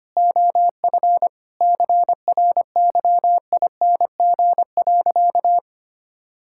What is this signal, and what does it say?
Morse code, heard as a keyed tone, about 25 wpm, 700 Hz. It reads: OF CRYING.
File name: practice / 25wpm / Sentences with 4-Word Phrases - 25wpm